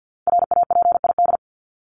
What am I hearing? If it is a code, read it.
Morse code, heard as a keyed tone, about 50 wpm, 700 Hz. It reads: RAPID
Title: LY Suffix - Top 100 Words with Base Words 50wpm